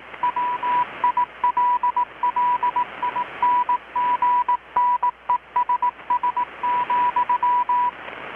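Morse code, heard as a keyed tone, about 18 wpm, 980 Hz, with radio noise. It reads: WILLINGNESS,
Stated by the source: Morse from KPH's Night of Nights XX maritime radiotelegraphy event, received in Chiapas, Mexico about 3,800 km away on a Kenwood R-600 with a grounded 10 m long-wire antenna.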